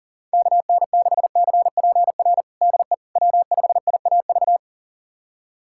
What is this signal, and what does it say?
Morse code, heard as a keyed tone, about 40 wpm, 700 Hz. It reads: KN6CPR DE W5IAV
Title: USA Callsign DE Callsign 40wpm